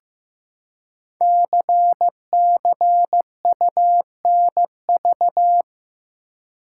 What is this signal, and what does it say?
Morse code, heard as a keyed tone, about 15 wpm, 700 Hz. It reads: CCUNV
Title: Code Groups - 5 Letters - 15wpm